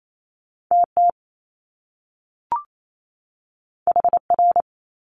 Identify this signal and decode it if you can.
Morse code, heard as a keyed tone, about 28 wpm, 700 Hz. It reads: TT HR
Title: Top Two Letter Combinations in Words 28wpm